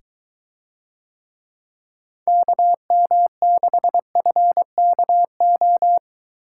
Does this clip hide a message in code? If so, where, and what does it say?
Yes: Morse code from 2.3 to 6.0 s, heard as a keyed tone, about 23 wpm, 700 Hz. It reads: KM6FKO